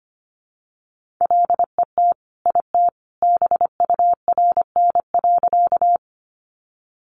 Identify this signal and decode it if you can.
Morse code, heard as a keyed tone, about 25 wpm, 700 Hz. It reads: LET IT BURN.
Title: Sentences from Top 1300 Words 25wpm